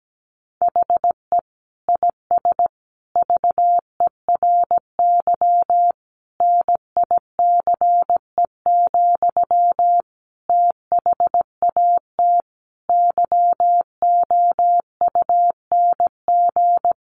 Morse code, heard as a keyed tone, about 17 wpm, 700 Hz. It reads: HE IS VERY NICE, THAT YOUNG
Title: Sentences from Top 1100 Words 17wpm